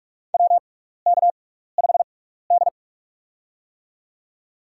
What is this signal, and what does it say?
Morse code, heard as a keyed tone, about 42 wpm, spaced as for 15 wpm, 700 Hz. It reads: WK5D
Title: USA Call Signs - 6x Character Spacing - 45wpm